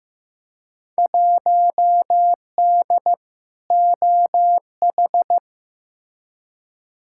Morse code, heard as a keyed tone, about 15 wpm, 700 Hz. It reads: 1D OH
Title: Field Day Exchanges - 15wpm